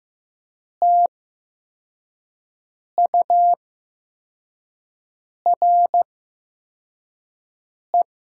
Morse code, heard as a keyed tone, about 15 wpm, spaced as for 4 wpm, 700 Hz. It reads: TURE
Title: 8x Character Spacing - Top 300 Words - 15wpm